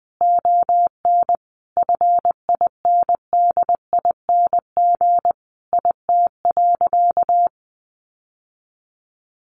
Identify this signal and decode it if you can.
Morse code, heard as a keyed tone, about 20 wpm, 700 Hz. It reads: ON FINDING IT.
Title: Sentences from Top 1100 Words 20wpm